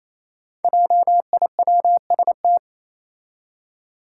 Morse code, heard as a keyed tone, about 28 wpm, 700 Hz. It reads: JIWST
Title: Five Challenging Letter-Number Combinations 28wpm